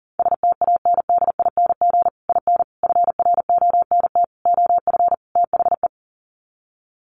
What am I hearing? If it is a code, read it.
Morse code, heard as a keyed tone, about 40 wpm, 700 Hz. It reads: STANDING IN FRONT OF THE
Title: Sets of 5 Words Encore2 40wpm